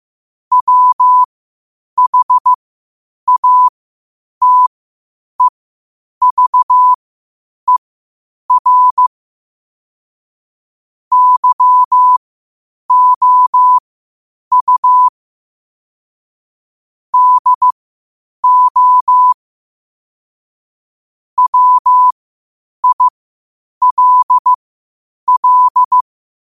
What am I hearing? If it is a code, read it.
Morse code, heard as a keyed tone, about 15 wpm, spaced as for 8 wpm, 1000 Hz. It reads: WHATEVER YOU DO WILL
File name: qotdfiles / 8wpm